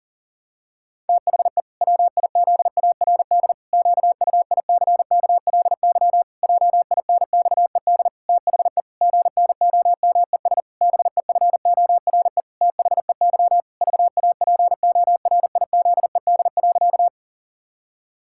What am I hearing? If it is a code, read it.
Morse code, heard as a keyed tone, about 37 wpm, 700 Hz. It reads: THE WIZARD QUICKLY JINXED THE GNOMES BEFORE THEY VAPORIZED.
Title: Pangram Challenge v2 - 40wpm